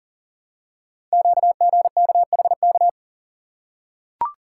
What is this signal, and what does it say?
Morse code, heard as a keyed tone, about 40 wpm, 700 Hz. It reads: QGKHK
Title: Code Groups - 5 Letters - 40wpm